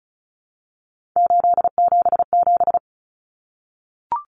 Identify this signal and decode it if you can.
Morse code, heard as a keyed tone, about 34 wpm, 700 Hz. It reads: 877
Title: Common Cut Numbers - 3-Digit 35wpm